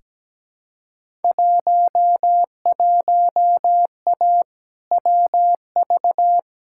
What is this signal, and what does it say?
Morse code, heard as a keyed tone, about 17 wpm, 700 Hz. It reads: 11A WV
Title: Field Day Exchanges Encore - 17wpm